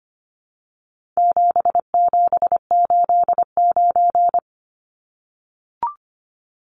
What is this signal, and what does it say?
Morse code, heard as a keyed tone, about 25 wpm, 700 Hz. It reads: 7789
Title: Numbers - 4-Digit 25wpm